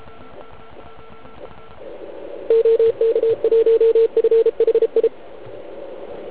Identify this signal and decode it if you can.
Morse code, heard as a keyed tone, about 33 wpm, 450 Hz. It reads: OK1FHI
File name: Archiv / CUC149 / OK1IF CUC149